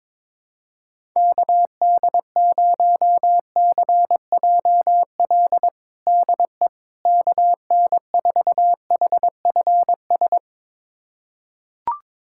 Morse code, heard as a keyed tone, about 21 wpm, 700 Hz. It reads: KD0CJL DE KN4HFS
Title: USA Callsign DE Callsign 22wpm